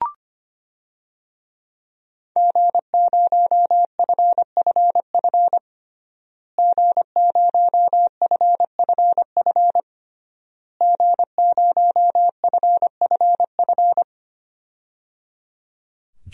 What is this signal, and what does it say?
Morse code, heard as a keyed tone, about 25 wpm, 700 Hz. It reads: G0FFF G0FFF G0FFF
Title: DXCC-100 Call Signs - 5 Characters - Repeated 3x - 25wpm